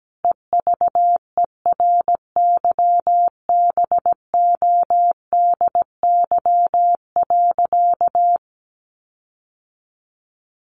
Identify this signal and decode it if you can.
Morse code, heard as a keyed tone, about 17 wpm, 700 Hz. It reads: EVERYBODY.
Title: Sentences from Top 1400 Words 17wpm